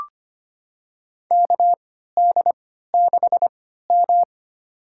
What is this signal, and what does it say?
Morse code, heard as a keyed tone, about 25 wpm, spaced as for 14 wpm, 700 Hz. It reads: KD6M
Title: USA Call Signs - 4 Characters - 3x Character Spacing - 25wpm